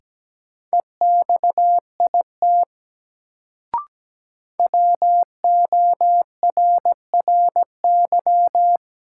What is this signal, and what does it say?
Morse code, heard as a keyed tone, about 17 wpm, 700 Hz. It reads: EXIT WORRY
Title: Sentences from Top 1100 Words - New Word Drill - 17wpm